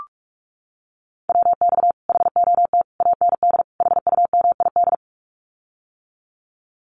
Morse code, heard as a keyed tone, about 44 wpm, 700 Hz. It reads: WX HOT AND HUMID
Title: Common QSO Phrases Encore 45wpm